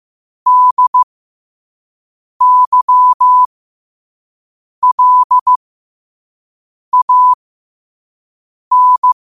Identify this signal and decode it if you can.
Morse code, heard as a keyed tone, about 15 wpm, spaced as for 5 wpm, 1000 Hz. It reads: DYLAN
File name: qotdfiles / 5wpm